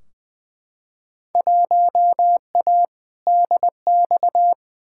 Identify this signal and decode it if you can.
Morse code, heard as a keyed tone, about 20 wpm, 700 Hz. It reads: 1A DX
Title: Field Day Exchanges Encore - 20wpm